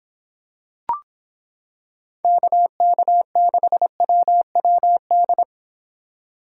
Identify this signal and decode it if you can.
Morse code, heard as a keyed tone, about 26 wpm, 700 Hz. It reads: KK6WWD